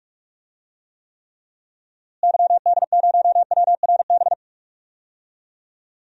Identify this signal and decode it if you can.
Morse code, heard as a keyed tone, about 46 wpm, 700 Hz. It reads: YD0WRB
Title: DXCC-100 Call Signs - 6 Characters - Repeated 3x - 45wpm